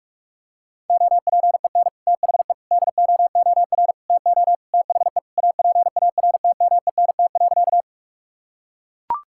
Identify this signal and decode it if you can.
Morse code, heard as a keyed tone, about 41 wpm, 700 Hz. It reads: OPEN THE DOOR TO THE APARTMENT.
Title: Sentences from Top 1000 Words - Review Part 6 - 45wpm